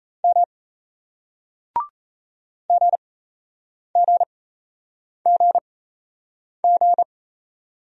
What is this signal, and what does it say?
Morse code, heard as a keyed tone, about 38 wpm, spaced as for 8 wpm, 700 Hz. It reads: M GGGG